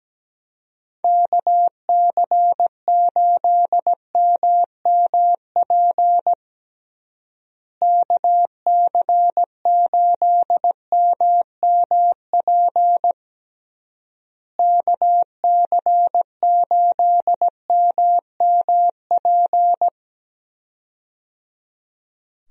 Morse code, heard as a keyed tone, about 17 wpm, 700 Hz. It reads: KC8MMP KC8MMP KC8MMP
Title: USA Call Signs - 6 Characters - Repeated 3x - 17wpm